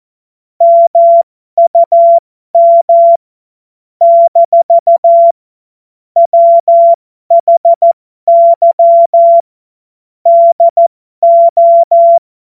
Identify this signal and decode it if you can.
Morse code, heard as a keyed tone, about 14 wpm, spaced as for 12 wpm, 680 Hz. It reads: MUM - WHY DO
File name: Morsekurs / Witze